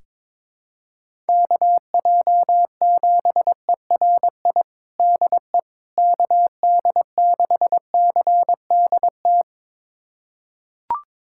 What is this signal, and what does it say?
Morse code, heard as a keyed tone, about 22 wpm, 700 Hz. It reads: KJ7ERI DE KD6CDT
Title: USA Callsign DE Callsign 22wpm